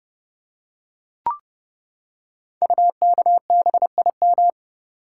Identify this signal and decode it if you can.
Morse code, heard as a keyed tone, about 28 wpm, 700 Hz. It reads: UKBIM
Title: Code Groups - 5 Letters - 30wpm